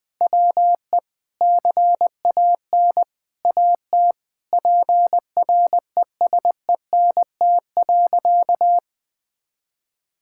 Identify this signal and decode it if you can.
Morse code, heard as a keyed tone, about 20 wpm, 700 Hz. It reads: WE CAN AT PRESENT.
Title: Sentences from Top 1000 Words - Review Part 5 - 20wpm